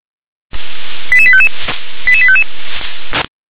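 Five-tone selective call (ZVEI), heard as quick runs of five tones, tones of about 70 ms. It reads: 88855 88855
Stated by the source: ZVEI five-tone call 88148 received with an SDR.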